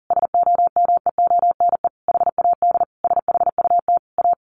Morse code, heard as a keyed tone, about 40 wpm, 700 Hz. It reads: SOMEONE HAD SHUT A
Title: A Sound Of Thunder 40wpm